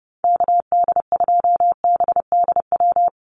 Morse code, heard as a keyed tone, about 30 wpm, 700 Hz. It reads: KD2BDW
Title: USA Call Signs Repeated 3x 30wpm